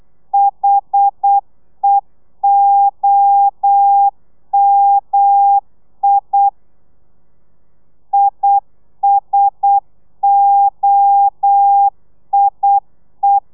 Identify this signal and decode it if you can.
Morse code, heard as a keyed tone, about 8 wpm, 800 Hz. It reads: HEOMI ISOIE
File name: telegrafia / i2tfj / Lez1 35